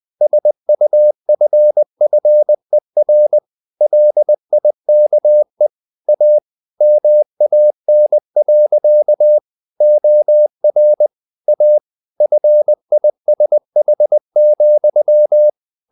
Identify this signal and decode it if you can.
Morse code, heard as a keyed tone, about 20 wpm, 600 Hz. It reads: SUFFER LIKE A MAN. OR A FISH,
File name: the old man and the sea part 2 20wpm